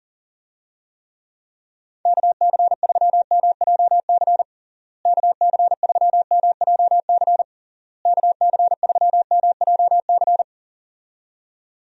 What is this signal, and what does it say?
Morse code, heard as a keyed tone, about 40 wpm, 700 Hz. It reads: KC3MJC KC3MJC KC3MJC
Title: USA Call Signs - 6 Characters - Repeated 3x - 40wpm